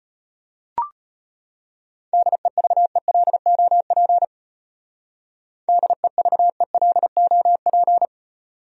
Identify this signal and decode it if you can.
Morse code, heard as a keyed tone, about 32 wpm, 700 Hz. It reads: DEVELOP DEVELOP